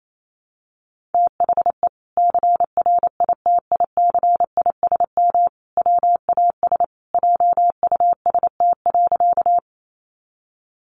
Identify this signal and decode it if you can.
Morse code, heard as a keyed tone, about 28 wpm, 700 Hz. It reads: THE CRITICISM WAS JUST.
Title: Sentences from Top 1300 Words 28wpm